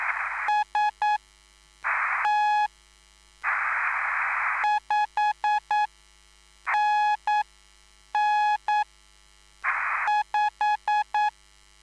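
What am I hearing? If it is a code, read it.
Morse code, heard as a keyed tone, about 9 wpm, spaced as for 6 wpm, 840 Hz. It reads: ST 5NN 5
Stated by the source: Live CW QSO between PA2BX and SP3JFK on the 14055 kHz QRS frequency, at a basic 8 wpm.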